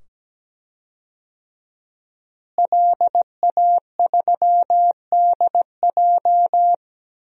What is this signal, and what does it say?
Morse code, heard as a keyed tone, about 17 wpm, 700 Hz. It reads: LA3DJ
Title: DXCC-100 Call Signs - 5 Characters - Repeated 2x - 17wpm